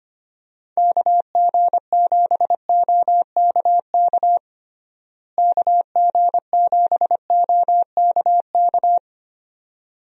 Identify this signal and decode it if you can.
Morse code, heard as a keyed tone, about 25 wpm, 700 Hz. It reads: KG7OKK KG7OKK
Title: USA Call Signs - 6 Characters - Repeated 2x - 25wpm